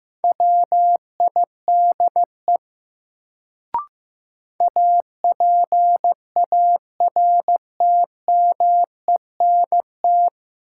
Morse code, heard as a keyed tone, about 15 wpm, 700 Hz. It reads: WIDE APARTMENT
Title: Sentences from Top 900 Words - New Word Drill - 15wpm